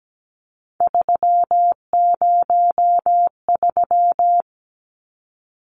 Morse code, heard as a keyed tone, about 17 wpm, 700 Hz. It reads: 303
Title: Numbers - 3-Digit 17wpm